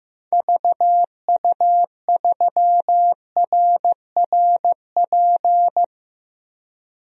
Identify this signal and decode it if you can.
Morse code, heard as a keyed tone, about 15 wpm, 700 Hz. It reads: VU3RRP
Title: DXCC-100 Call Signs - 6 Characters - Repeated 2x - 15wpm